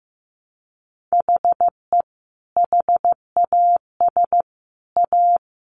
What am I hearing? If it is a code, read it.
Morse code, heard as a keyed tone, about 15 wpm, 700 Hz. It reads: HE HAS A
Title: Sentences from Top 1100 Words 15wpm